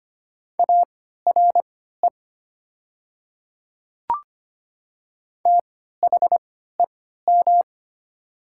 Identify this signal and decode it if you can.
Morse code, heard as a keyed tone, about 25 wpm, spaced as for 14 wpm, 690 Hz. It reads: ARE THEM
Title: 3x Character Spacing - Top 100 Words - 25wpm